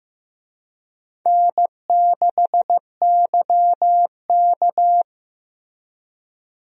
Morse code, heard as a keyed tone, about 15 wpm, 700 Hz. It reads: N6YK
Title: USA Call Signs - 4 Characters - Repeated 3x - 15wpm